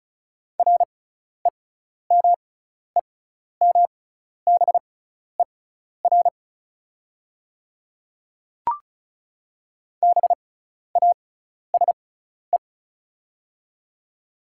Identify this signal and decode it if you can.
Morse code, heard as a keyed tone, about 36 wpm, spaced as for 12 wpm, 690 Hz. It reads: REMEMBER BASE
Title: 6x Character Spacing - Top 500 Words - 35wpm